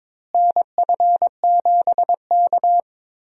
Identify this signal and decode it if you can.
Morse code, heard as a keyed tone, about 22 wpm, 700 Hz. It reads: NF7K
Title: USA Call Signs - 4 Characters - Repeated 2x - 22wpm